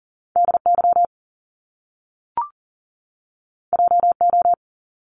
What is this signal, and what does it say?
Morse code, heard as a keyed tone, about 38 wpm, 700 Hz. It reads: DY JO
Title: Top Two Letter Combinations in Words 40wpm